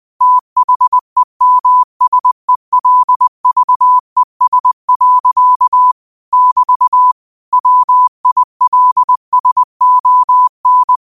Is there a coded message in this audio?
Yes: Morse code from 0.2 to 11.0 s, heard as a keyed tone, about 20 wpm, 1000 Hz. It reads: THEMSELVES. <BT> WILSON